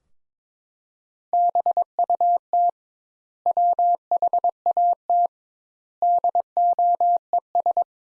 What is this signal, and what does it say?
Morse code, heard as a keyed tone, about 22 wpm, 700 Hz. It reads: BUT WHAT DOES
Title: Sentences Top 200 Words - 22wpm